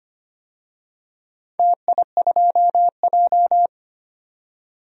Morse code, heard as a keyed tone, about 25 wpm, 700 Hz. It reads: TI2J
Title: DXCC-100 Call Signs - 4 Characters - 25wpm